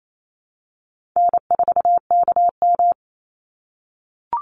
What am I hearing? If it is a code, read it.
Morse code, heard as a keyed tone, about 28 wpm, 700 Hz. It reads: N4KM
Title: Easy Call Signs - 28wpm